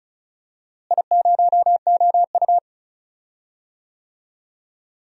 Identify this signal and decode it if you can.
Morse code, heard as a keyed tone, about 36 wpm, 700 Hz. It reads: I0OU
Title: DXCC-100 Call Signs - 4 Characters - Repeated 2x - 35wpm